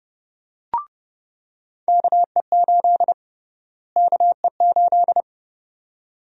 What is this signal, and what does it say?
Morse code, heard as a keyed tone, about 30 wpm, 700 Hz. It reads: KE8 KE8
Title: USA Call Sign Prefixes - Repeated 2x - 30wpm